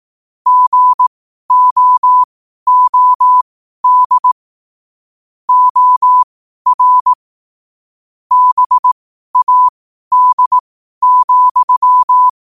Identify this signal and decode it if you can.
Morse code, heard as a keyed tone, about 18 wpm, spaced as for 12 wpm, 1000 Hz. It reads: GOOD OR BAD,
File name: qotdfiles / 12wpm